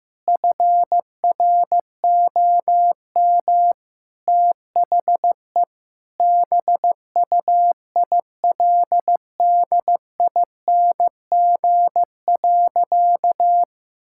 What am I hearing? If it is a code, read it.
Morse code, heard as a keyed tone, about 15 wpm, 700 Hz. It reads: FROM THE BUILDING.
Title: Sentences from Top 1000 Words - Review Part 5 - 15wpm